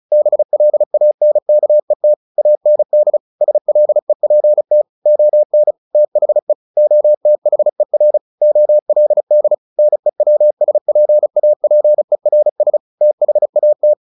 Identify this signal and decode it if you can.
Morse code, heard as a keyed tone, about 35 wpm, 600 Hz. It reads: BLANKET AND SLEPT ON THE OTHER OLD NEWSPAPERS THAT
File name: the old man and the sea / the old man and the sea 1 35wpm